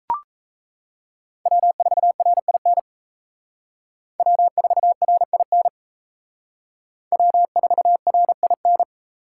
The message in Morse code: W4RIN W4RIN W4RIN

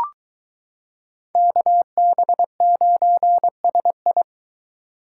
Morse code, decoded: KB9SI

